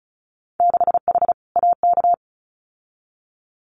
Morse code, heard as a keyed tone, about 36 wpm, 700 Hz. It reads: 6H AK
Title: Winter Field Day Exchanges Encore 35wpm